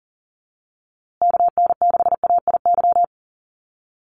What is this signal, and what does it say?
Morse code, heard as a keyed tone, about 40 wpm, 700 Hz. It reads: KN6AIY
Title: USA Call Signs Repeated 2x 40wpm